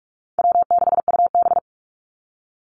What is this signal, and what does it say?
Morse code, heard as a keyed tone, about 44 wpm, 700 Hz. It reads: W6UB
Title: NA SOTA Activator Callsigns 45wpm